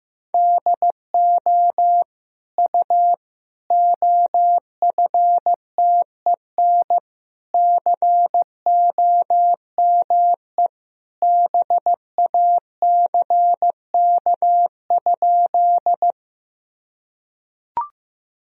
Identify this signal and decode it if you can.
Morse code, heard as a keyed tone, about 15 wpm, 700 Hz. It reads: DO U OFTEN COME BACK?
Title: Sentences from Top 300 Words and Abbreviations 15wpm